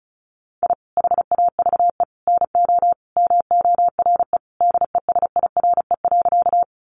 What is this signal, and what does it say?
Morse code, heard as a keyed tone, about 34 wpm, 700 Hz. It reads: I HAVE NO MORE DESIRE.
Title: Sentences from Top 1100 Words 35wpm